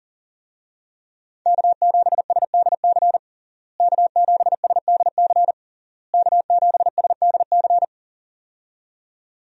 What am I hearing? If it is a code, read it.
Morse code, heard as a keyed tone, about 40 wpm, 700 Hz. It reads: K7SDC K7SDC K7SDC
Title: USA Call Signs - 5 Characters - Repeated 3x - 40wpm